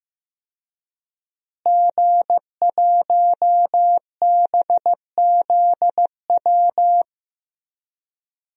Morse code, heard as a keyed tone, about 15 wpm, 700 Hz. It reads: G1BZW
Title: DXCC-100 Call Signs - 5 Characters - Repeated 3x - 15wpm